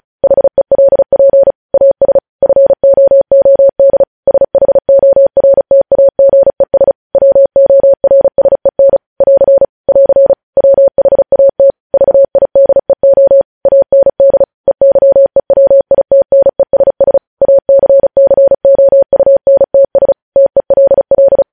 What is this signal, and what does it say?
Morse code, heard as a keyed tone, about 36 wpm, 550 Hz. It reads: HELP AS FOOD SHORTAGES WORSEN <AR> <AR> WHAT VIDEO AND EYEWITNESS ACCOUNTS TELL
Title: News Headlines at 35